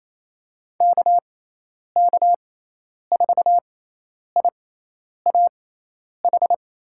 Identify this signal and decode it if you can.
Morse code, heard as a keyed tone, about 28 wpm, spaced as for 10 wpm, 700 Hz. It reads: KK4IAH
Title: USA Call Signs - 6 Characters - 6x Character Spacing - 28wpm